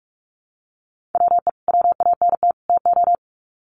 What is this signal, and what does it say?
Morse code, heard as a keyed tone, about 44 wpm, 700 Hz. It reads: WE WANT TO
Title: Sets of 3 Words 45wpm